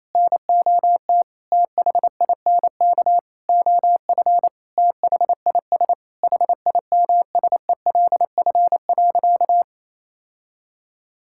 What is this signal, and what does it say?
Morse code, heard as a keyed tone, about 28 wpm, 700 Hz. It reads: NOT THINK OF THIS HIMSELF.
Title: Sentences from Top 1600 Words 28wpm